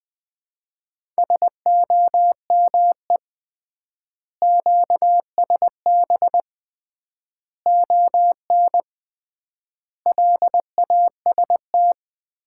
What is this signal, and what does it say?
Morse code, heard as a keyed tone, about 21 wpm, 700 Hz. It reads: SOME QSB ON LAST
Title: Common QSO Phrases - 3x Word-Spacing - 20wpm